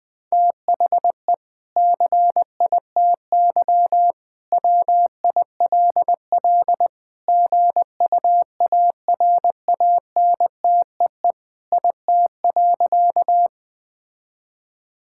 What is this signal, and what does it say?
Morse code, heard as a keyed tone, about 20 wpm, 700 Hz. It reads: THE CITY WILL GUARANTEE IT.
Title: Sentences from Top 1600 Words 20wpm